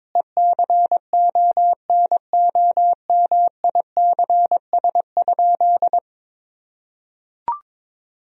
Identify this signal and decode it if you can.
Morse code, heard as a keyed tone, about 22 wpm, 700 Hz. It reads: ECONOMICS?
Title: Sentences from Top 1000 Words - Review Part 3 - 22wpm